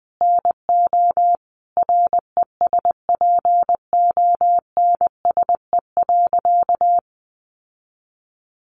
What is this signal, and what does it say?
Morse code, heard as a keyed tone, about 20 wpm, 700 Hz. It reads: NO RESPONSE.
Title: Sentences from Top 1400 Words 20wpm